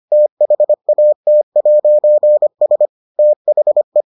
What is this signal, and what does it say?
Morse code, heard as a keyed tone, about 25 wpm, 600 Hz. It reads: THAT'S THE